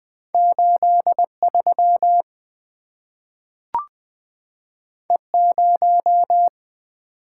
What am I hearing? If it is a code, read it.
Morse code, heard as a keyed tone, about 20 wpm, 700 Hz. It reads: 83 E0
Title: Two Letter-Number Combinations - 20wpm